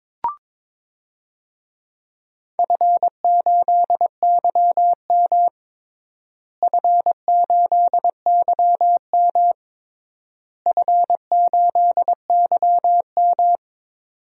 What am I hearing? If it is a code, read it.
Morse code, heard as a keyed tone, about 22 wpm, 700 Hz. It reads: F8YM F8YM F8YM
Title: DXCC-100 Call Signs - 4 Characters - Repeated 3x - 22wpm